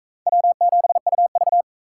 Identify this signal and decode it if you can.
Morse code, heard as a keyed tone, about 39 wpm, 700 Hz. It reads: W7UV